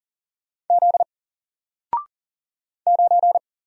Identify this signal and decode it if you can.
Morse code, heard as a keyed tone, about 39 wpm, 700 Hz. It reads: Z 9
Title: Single Letter-Number 40wpm